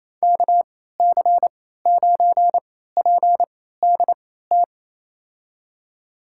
Morse code, heard as a keyed tone, about 28 wpm, spaced as for 16 wpm, 700 Hz. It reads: KC9PDT
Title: USA Call Signs - 3x Character Spacing - 28wpm